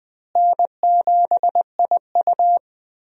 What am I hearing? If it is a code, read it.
Morse code, heard as a keyed tone, about 20 wpm, 700 Hz. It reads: N7IU
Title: USA Call Signs - 4 Characters - Repeated 3x - 20wpm